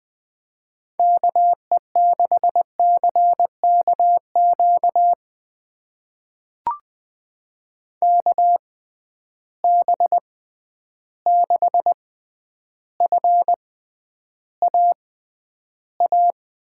Morse code, heard as a keyed tone, about 20 wpm, spaced as for 10 wpm, 700 Hz. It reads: KE6CKQ KB6FAA
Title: USA Call Signs - 6 Characters - 6x Character Spacing - 20wpm